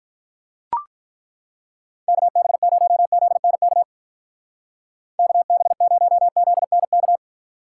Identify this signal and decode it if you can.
Morse code, heard as a keyed tone, about 42 wpm, 700 Hz. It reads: KB0ZNK KB0ZNK
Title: Speed Racing - USA 2x3 Call Signs 35wpm